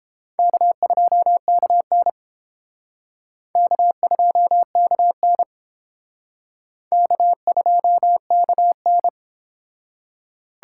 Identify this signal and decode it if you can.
Morse code, heard as a keyed tone, about 30 wpm, 700 Hz. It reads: K2KN K2KN K2KN